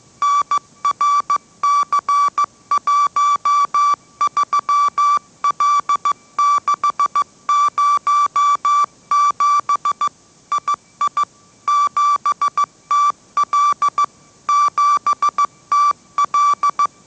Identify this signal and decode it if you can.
Morse code, heard as a keyed tone, about 15 wpm, 1160 Hz. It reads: NRC13L607 II 7TL 7TL
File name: mrx / mrx065 16wpm